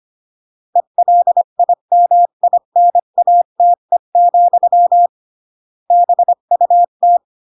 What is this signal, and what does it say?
Morse code, heard as a keyed tone, about 25 wpm, 700 Hz. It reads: ELIMINATE, BUT
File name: CW / jodorowsky / 82 commandments 25-20